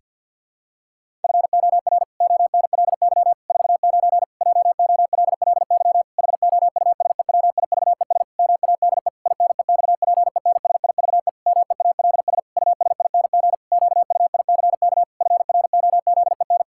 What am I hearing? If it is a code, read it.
Morse code, heard as a keyed tone, about 51 wpm, 700 Hz. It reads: FOR ONLY 49 JOLLY HOUSEWIVES MADE INEXPENSIVE MEALS USING QUICK FROZEN
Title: Pangram Challenge v2 - 50wpm